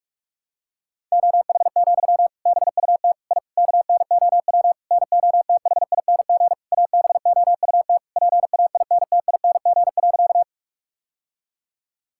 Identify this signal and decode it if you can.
Morse code, heard as a keyed tone, about 41 wpm, 700 Hz. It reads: OH, BUT I KNOW NOTHING ABOUT PAINTING.
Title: Sentences from Top 1000 Words - Review Part 6 - 45wpm